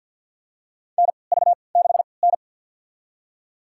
Morse code, heard as a keyed tone, about 52 wpm, spaced as for 28 wpm, 700 Hz. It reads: NV6N
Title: USA Call Signs - 4 Characters - 3x Character Spacing - 50wpm